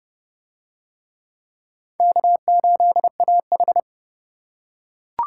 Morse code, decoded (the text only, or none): K8AH